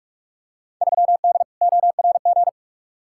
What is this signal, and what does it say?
Morse code, heard as a keyed tone, about 46 wpm, 700 Hz. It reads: 3D ORG
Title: Field Day Exchanges Encore - 45wpm